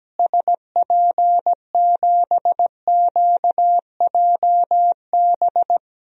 Morse code, heard as a keyed tone, about 17 wpm, 700 Hz. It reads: SP7QJB
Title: DXCC-100 Call Signs - 6 Characters - Repeated 3x - 17wpm